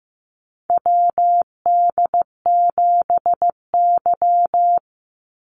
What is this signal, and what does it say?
Morse code, heard as a keyed tone, about 15 wpm, 700 Hz. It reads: WD7Y
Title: NA SOTA Chaser Callsigns 15wpm